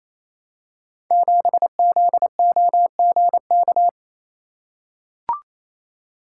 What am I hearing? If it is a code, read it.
Morse code, heard as a keyed tone, about 28 wpm, 700 Hz. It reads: 7ZOGK
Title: Code Groups - 5 Letter:Numbers - 28wpm